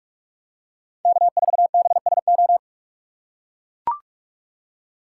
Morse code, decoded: K4BSO

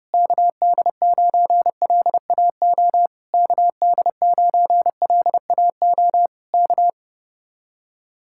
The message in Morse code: KD9LAO KD9LAO K

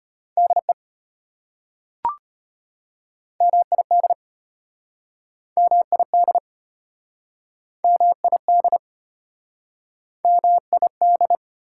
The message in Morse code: DE MID MID MID MID

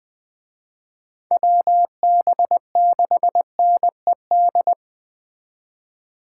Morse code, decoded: WB6NED

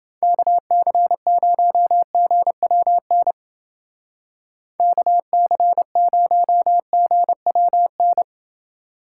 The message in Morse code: KC0GWN KC0GWN

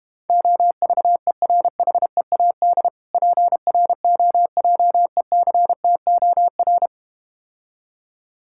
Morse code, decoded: OVERHEAD PROJECTOR

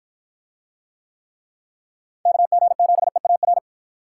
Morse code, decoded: KG7EAR

